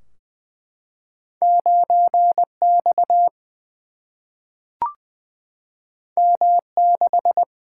Morse code, heard as a keyed tone, about 20 wpm, 700 Hz. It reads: 9 X M 6
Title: Single-Letter Single-Number 20wpm